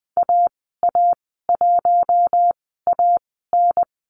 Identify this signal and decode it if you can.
Morse code, heard as a keyed tone, about 20 wpm, spaced as for 14 wpm, 700 Hz. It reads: AA1AN
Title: 055 Lesson 10g - Letter r - Call Signs 20wpm